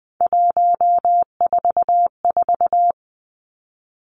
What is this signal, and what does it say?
Morse code, heard as a keyed tone, about 20 wpm, 700 Hz. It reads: 144
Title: Three Challenging Letter-Number Combinations 20wpm